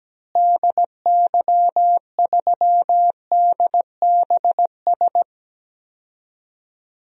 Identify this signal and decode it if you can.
Morse code, heard as a keyed tone, about 17 wpm, 700 Hz. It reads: DY3DBS
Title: DXCC-100 Call Signs - 6 Characters - Repeated 2x - 17wpm